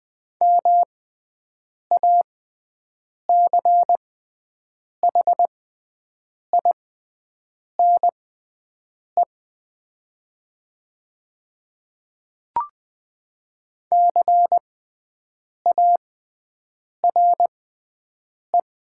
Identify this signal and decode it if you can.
Morse code, heard as a keyed tone, about 21 wpm, spaced as for 7 wpm, 700 Hz. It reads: MACHINE CARE
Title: 6x Character Spacing - Top 500 Words - 20wpm